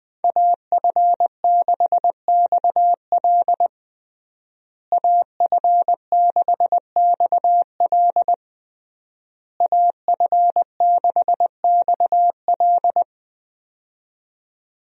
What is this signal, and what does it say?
Morse code, heard as a keyed tone, about 20 wpm, 700 Hz. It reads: AF6XL AF6XL AF6XL
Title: USA Call Signs - 5 Characters - Repeated 3x - 20wpm